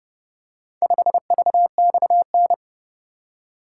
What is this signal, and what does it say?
Morse code, heard as a keyed tone, about 29 wpm, 700 Hz. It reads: 5VXN